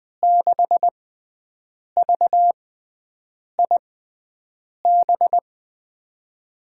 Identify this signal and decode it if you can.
Morse code, heard as a keyed tone, about 21 wpm, spaced as for 7 wpm, 700 Hz. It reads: 6VIB
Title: USA Call Signs - 5 Characters - 6x Character Spacing - 20wpm